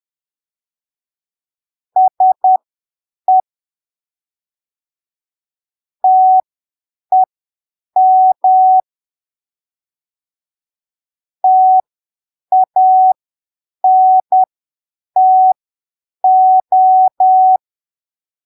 Morse code, heard as a keyed tone, about 10 wpm, spaced as for 7 wpm, 750 Hz. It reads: SE TEM TANTO
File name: arquivos / texto / texto 21 10ppm